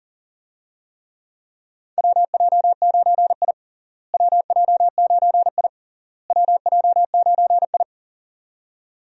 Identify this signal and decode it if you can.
Morse code, heard as a keyed tone, about 37 wpm, 700 Hz. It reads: WJ9I WJ9I WJ9I
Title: USA Call Signs - 4 Characters - Repeated 3x - 40wpm